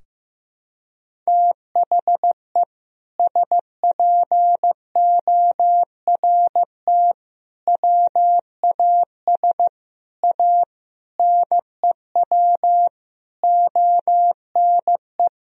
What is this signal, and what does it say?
Morse code, heard as a keyed tone, about 15 wpm, 700 Hz. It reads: THE SPORT WAS A NEW ONE
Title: Sentences from Top 1000 Words - Review Part 4 - 15wpm